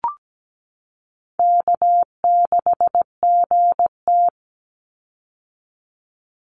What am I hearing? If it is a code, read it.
Morse code, heard as a keyed tone, about 17 wpm, 700 Hz. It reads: K6GT
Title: Call Signs - 17wpm